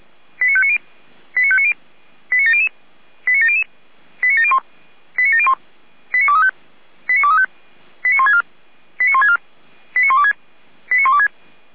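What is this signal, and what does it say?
Five-tone selective call (ZVEI), heard as quick runs of five tones, tones of about 70 ms. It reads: 89590 89590 89700 89700 89701 89701 89235 89235 89164 89164 89126 89126
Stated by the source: Received on SDR, ZVEI five-tone code 89094, 455 channel.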